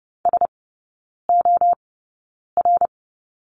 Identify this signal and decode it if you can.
Morse code, heard as a keyed tone, about 30 wpm, spaced as for 9 wpm, 690 Hz. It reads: SOR
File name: Lesson 17e - Letter u - Three Characters 30wpm